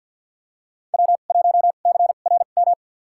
Morse code, heard as a keyed tone, about 52 wpm, spaced as for 36 wpm, 700 Hz. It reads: W1CRM
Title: USA Call Signs - 2x Character Spacing - 50wpm